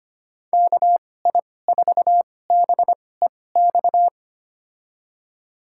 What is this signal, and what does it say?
Morse code, heard as a keyed tone, about 25 wpm, spaced as for 18 wpm, 700 Hz. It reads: KI4BEX
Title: USA Call Signs - 6 Characters - 2x Character Spacing - 25wpm